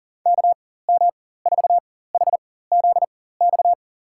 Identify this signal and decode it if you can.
Morse code, heard as a keyed tone, about 40 wpm, spaced as for 19 wpm, 700 Hz. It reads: KM4HZX
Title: USA Call Signs - 6 Characters - 4x Character Spacing - 40wpm